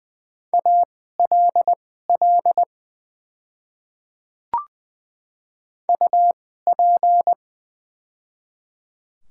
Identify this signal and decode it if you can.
Morse code, heard as a keyed tone, about 21 wpm, spaced as for 15 wpm, 700 Hz. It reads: ALL UP
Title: 2x Character Spacing - Top 100 Words - 20wpm